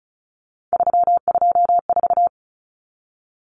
Morse code, heard as a keyed tone, about 34 wpm, 700 Hz. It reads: 324
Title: Numbers - 3-Digit 35wpm